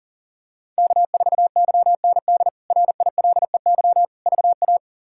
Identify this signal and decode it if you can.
Morse code, heard as a keyed tone, about 37 wpm, 700 Hz. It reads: K4YND RILEY VA
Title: K1USN SST QSO Practice - 40wpm